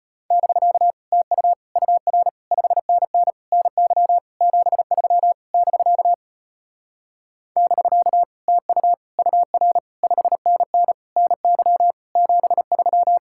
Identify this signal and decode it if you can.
Morse code, heard as a keyed tone, about 33 wpm, 700 Hz. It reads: <BK> TU UR 5NN NY 73 <BK> <BK> TU UR 5NN NY 73